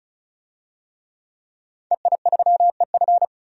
Morse code, heard as a keyed tone, about 36 wpm, 700 Hz. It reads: EI3EF
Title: DXCC-100 Call Signs - 5 Characters - 35wpm